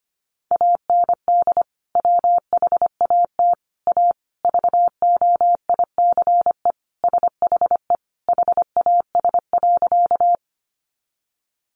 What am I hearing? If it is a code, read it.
Morse code, heard as a keyed tone, about 25 wpm, 700 Hz. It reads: AND WHAT A VOICE SHE HAS.
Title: Sentences from Top 500 Words 25wpm